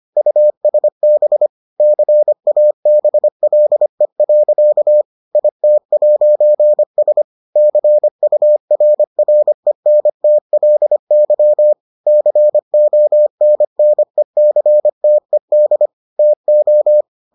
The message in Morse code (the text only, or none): USB CABLE. IT'S CURRENTLY CONNECTED TO